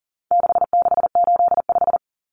Morse code, heard as a keyed tone, about 40 wpm, 700 Hz. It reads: 6685
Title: Common Cut Numbers - 4-Digit 40wpm